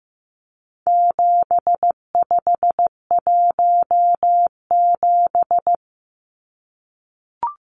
Morse code, heard as a keyed tone, about 15 wpm, 700 Hz. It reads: 7517
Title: Numbers - 4-Digit 15wpm